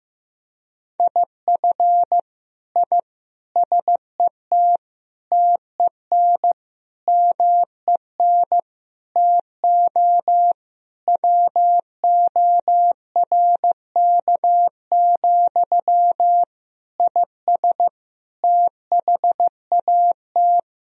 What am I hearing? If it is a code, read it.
Morse code, heard as a keyed tone, about 16 wpm, 700 Hz. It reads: IF I SET TEN MEN TO WORK, IS THAT
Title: Sentences from Top 1000 Words - Review Part 2 - 15wpm